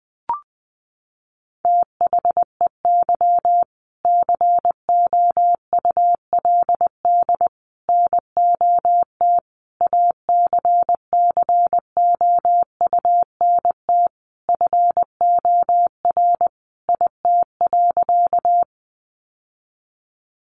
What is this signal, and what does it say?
Morse code, heard as a keyed tone, about 20 wpm, 700 Hz. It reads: THEY COULD NOT ACCOUNT FOR IT.
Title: Sentences from Top 900 Words 20wpm